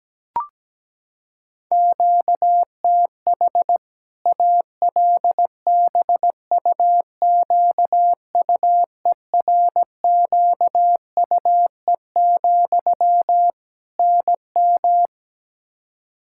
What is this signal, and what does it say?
Morse code, heard as a keyed tone, about 17 wpm, 700 Hz. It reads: QTH ALBUQUERQUE, NM
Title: Practice QSO - Exchange 1 - Beginning - 17wpm